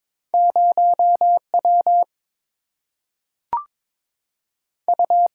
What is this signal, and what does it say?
Morse code, heard as a keyed tone, about 22 wpm, 700 Hz. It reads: 0W U
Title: Single-Letter Single-Number 22wpm